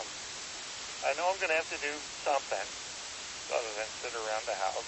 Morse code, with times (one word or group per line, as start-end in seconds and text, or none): none